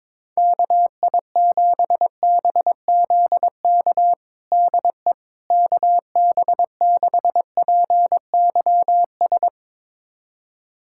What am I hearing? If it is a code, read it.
Morse code, heard as a keyed tone, about 22 wpm, 700 Hz. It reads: KI7BZK DE KB6PYS
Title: USA Callsign DE Callsign - 6 Characters - 22wpm